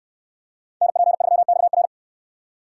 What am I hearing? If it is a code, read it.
Morse code, heard as a keyed tone, about 69 wpm, 700 Hz. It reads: NP3XR